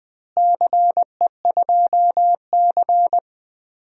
Morse code, heard as a keyed tone, about 20 wpm, 700 Hz. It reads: CE2C
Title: DXCC-100 Call Signs - 4 Characters - Repeated 2x - 20wpm